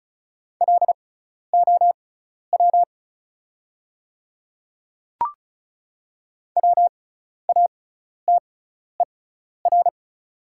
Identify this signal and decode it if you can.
Morse code, heard as a keyed tone, about 36 wpm, spaced as for 12 wpm, 690 Hz. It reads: LOW WATER
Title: 6x Character Spacing - Top 200 Words - 35wpm